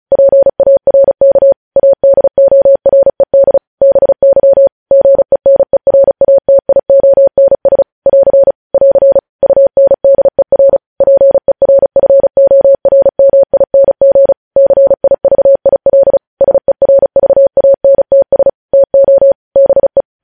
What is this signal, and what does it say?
Morse code, heard as a keyed tone, about 36 wpm, 550 Hz. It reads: PARK ADORED BY GENERATIONS <AR> <AR> UNDER PERFORMING CIVIL SERVANTS TO BE
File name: podcast / News headlines-1x-35x35